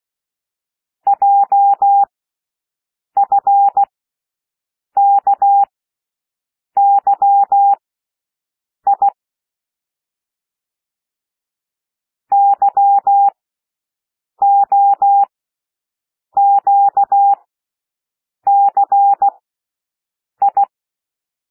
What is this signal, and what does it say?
Morse code, heard as a keyed tone, about 16 wpm, spaced as for 6 wpm, 800 Hz. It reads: JFKYI YOQCI
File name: cwmp3 / L1 SOUND LETTERS 16 wpm